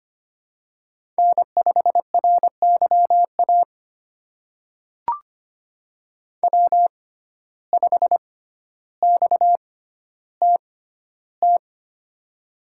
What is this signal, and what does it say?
Morse code, heard as a keyed tone, about 25 wpm, spaced as for 13 wpm, 700 Hz. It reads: N5RYA W5XTT
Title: USA Call Signs - 6x Character Spacing - 25wpm